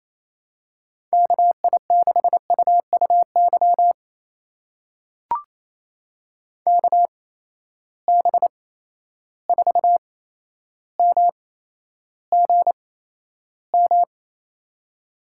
Morse code, heard as a keyed tone, about 28 wpm, spaced as for 12 wpm, 700 Hz. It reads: KI6UUY KB4MGM
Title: USA Call Signs - 6 Characters - 8x Character Spacing - 28wpm